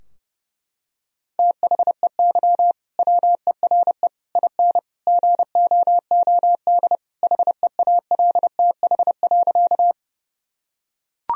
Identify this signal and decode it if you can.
Morse code, heard as a keyed tone, about 30 wpm, 700 Hz. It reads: THEY WERE IN GOOD HEALTH.
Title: Sentences from Top 1000 Words - Review Part 3 - 30wpm